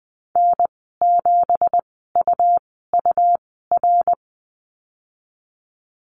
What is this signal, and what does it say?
Morse code, heard as a keyed tone, about 20 wpm, spaced as for 15 wpm, 700 Hz. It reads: N7UUR
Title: 167 Lesson 27g - Number 7 - Call Signs 20wpm